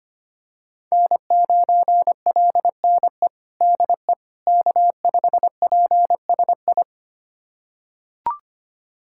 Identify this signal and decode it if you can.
Morse code, heard as a keyed tone, about 25 wpm, 700 Hz. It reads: N9LNE DE K5PSI
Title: USA Callsign DE Callsign - 5 Characters - 25wpm